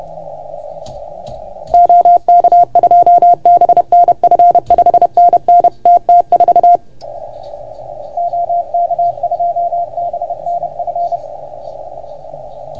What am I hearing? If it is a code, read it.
Morse code, heard as a keyed tone, about 31 wpm, 690 Hz. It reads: OK2BNF 5NN TT4 OK2BNF
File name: Archiv / CUC381 / OK1FGD CUC 381